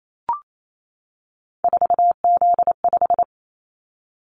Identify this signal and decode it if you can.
Morse code, heard as a keyed tone, about 28 wpm, 700 Hz. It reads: 4Z5
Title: Callsign Prefixes for Top 50 DXCC - 28wpm